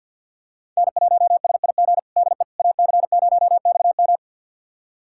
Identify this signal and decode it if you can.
Morse code, heard as a keyed tone, about 51 wpm, 700 Hz. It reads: N1SIG DE AC0XM
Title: USA Callsign DE Callsign - 5 Characters - 50wpm